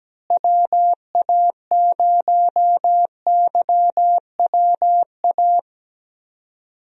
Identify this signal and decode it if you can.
Morse code, heard as a keyed tone, about 17 wpm, 700 Hz. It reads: WA0YWA
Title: USA Call Signs - 6 Characters - Repeated 2x - 17wpm